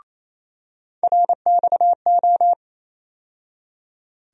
Morse code, heard as a keyed tone, about 28 wpm, 700 Hz. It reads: RXO
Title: USA Call Sign Postfixes - 3 Letters 28wpm